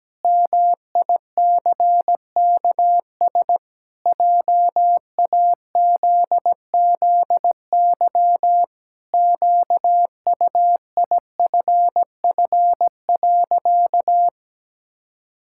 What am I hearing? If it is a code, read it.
Morse code, heard as a keyed tone, about 17 wpm, 700 Hz. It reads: MICKS JAZZY QUIFF.
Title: Pangram Challenge v2 - 17wpm